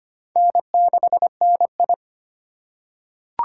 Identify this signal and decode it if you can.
Morse code, heard as a keyed tone, about 25 wpm, 700 Hz. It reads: N6NI
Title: USA Call Signs - 4x Character Spacing - 25wpm